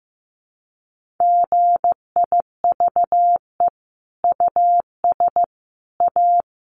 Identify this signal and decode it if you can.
Morse code, heard as a keyed tone, about 15 wpm, 700 Hz. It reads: GIVE US A
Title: Sentences from Top 1500 Words 15wpm